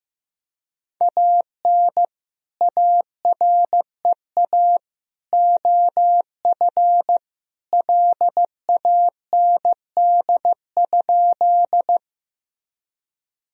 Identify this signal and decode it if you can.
Morse code, heard as a keyed tone, about 15 wpm, 700 Hz. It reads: AN AREA OF LAND?
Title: Sentences from Top 1000 Words - Review Part 2 - 15wpm